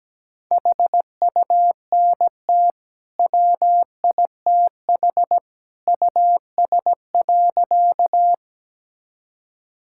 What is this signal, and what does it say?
Morse code, heard as a keyed tone, about 17 wpm, 700 Hz. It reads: HUNT WITH US.
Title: Sentences from Top 1600 Words 17wpm